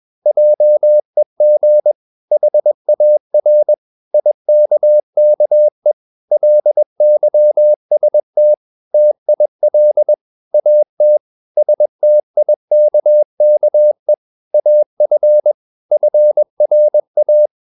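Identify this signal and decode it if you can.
Morse code, heard as a keyed tone, about 21 wpm, 600 Hz. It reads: JEG HAR IKKE LYST TIL AT STIKKE AF FRA
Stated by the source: Morse practice at 21 wpm, slow-speed holiday exercise.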